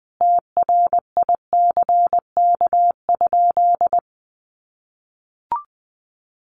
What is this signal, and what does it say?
Morse code, heard as a keyed tone, about 20 wpm, 700 Hz. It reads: TRICK?
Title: Sentences from Top 1500 Words 20wpm